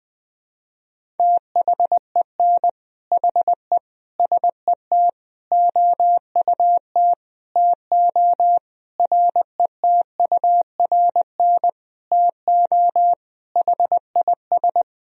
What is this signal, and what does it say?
Morse code, heard as a keyed tone, about 20 wpm, 700 Hz. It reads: THEN HE SET OUT TO RETURN TO HIS
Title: Sentences from Top 1000 Words - Review Part 1 - 20wpm